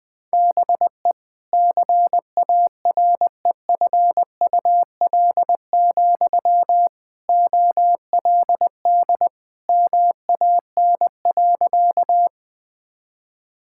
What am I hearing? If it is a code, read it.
Morse code, heard as a keyed tone, about 20 wpm, 700 Hz. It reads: BE CAREFUL, OLD MAN.
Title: Sentences from Top 1600 Words 20wpm